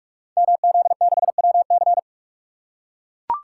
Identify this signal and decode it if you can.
Morse code, heard as a keyed tone, about 44 wpm, 700 Hz. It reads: MZ6WC
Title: Code Groups - 5 Letter:Numbers - 45wpm